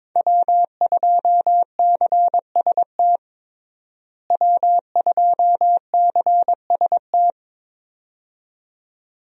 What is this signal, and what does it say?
Morse code, heard as a keyed tone, about 22 wpm, 700 Hz. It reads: W2CST W2CST
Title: USA Call Signs - 5 Characters - Repeated 2x - 22wpm